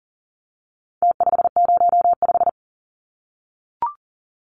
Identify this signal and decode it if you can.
Morse code, heard as a keyed tone, about 40 wpm, 700 Hz. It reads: T505
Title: Numbers - Contest Style - 4-Digit 40wpm